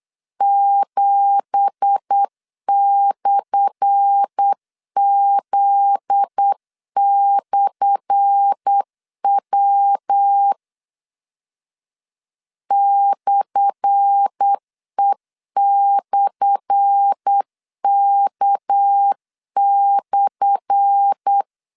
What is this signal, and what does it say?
Morse code, heard as a keyed tone, about 8 wpm, 800 Hz. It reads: 7/Z/W /E/K/
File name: LEZIONI / 10 SIGNS / 10A SLASH 8wpm 3452